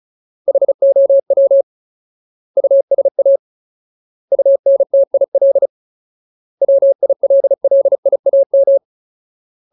Morse code, heard as a keyed tone, about 34 wpm, 550 Hz. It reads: HOW USA UNTIL WILLIAM
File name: Ultimate Word List / 35WPM Ultimate